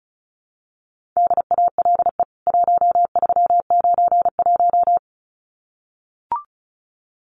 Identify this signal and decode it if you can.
Morse code, heard as a keyed tone, about 34 wpm, 700 Hz. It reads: DALE 1391
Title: CWT Exchange 35wpm